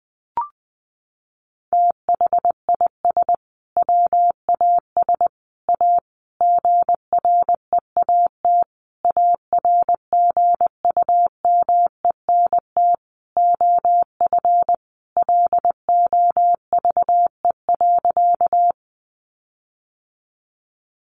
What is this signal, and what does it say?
Morse code, heard as a keyed tone, about 20 wpm, 700 Hz. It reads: THIS WAS A GREAT ARGUMENT OF LOVE.
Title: Sentences from Top 1400 Words 20wpm